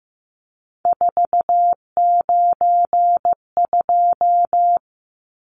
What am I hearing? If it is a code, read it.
Morse code, heard as a keyed tone, about 15 wpm, 700 Hz. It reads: 492
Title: Numbers - 3-Digit 15wpm